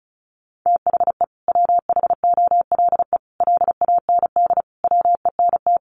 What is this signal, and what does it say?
Morse code, heard as a keyed tone, about 34 wpm, 700 Hz. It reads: THE WHOLE LAND WENT
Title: Sentences from Top 1400 Words 35wpm